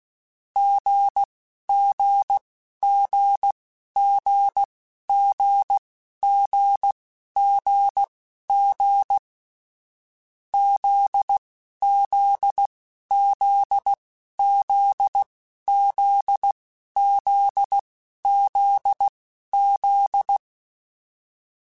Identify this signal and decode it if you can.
Morse code, heard as a keyed tone, about 16 wpm, spaced as for 11 wpm, 800 Hz. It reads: GGGGGGGG ZZZZZZZZ
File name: Short Runs / Short Runs-mp3s / GZ GZ7 @10